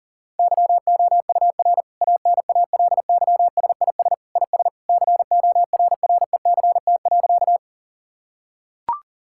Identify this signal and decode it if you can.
Morse code, heard as a keyed tone, about 40 wpm, 700 Hz. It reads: YOUR ANALYSIS IS CORRECT.
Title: Sentences from Top 1000 Words - Review Part 5 - 40wpm